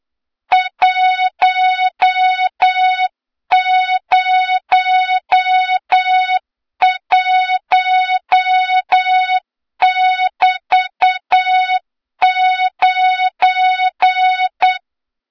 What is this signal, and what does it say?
Morse code, heard as a keyed tone, about 8 wpm, 3700 Hz, with noise. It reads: 101<BT>9
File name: Morze-HA2MN / hang / 23 GYAKORLAT